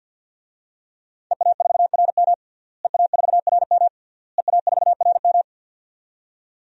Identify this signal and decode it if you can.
Morse code, heard as a keyed tone, about 52 wpm, 700 Hz. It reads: EA4RM EA4RM EA4RM
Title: DXCC-100 Call Signs - 5 Characters - Repeated 3x - 50wpm